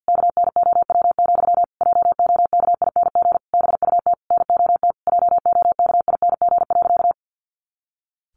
Morse code, heard as a keyed tone, about 50 wpm, 700 Hz. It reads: KNOW, JOKING BUT NOT JOKING.
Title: A Sound Of Thunder 50wpm